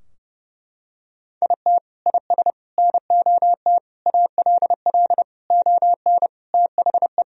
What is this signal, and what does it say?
Morse code, heard as a keyed tone, about 30 wpm, 700 Hz. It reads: IT IS NOT ALL ON THE
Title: Sentences from Top 1000 Words - Review Part 3 - 30wpm